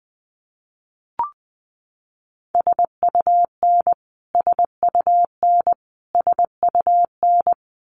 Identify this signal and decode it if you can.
Morse code, heard as a keyed tone, about 20 wpm, 700 Hz. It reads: SUN SUN SUN
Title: Top 500 Words Repeated 20wpm